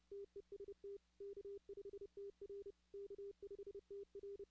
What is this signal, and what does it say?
Morse code, heard as a keyed tone, about 31 wpm, 400 Hz. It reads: TEST K5TR K5TR
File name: 2016 arrl dx cw / 2016-ARRL-DX-CW-K5TR- - 0-46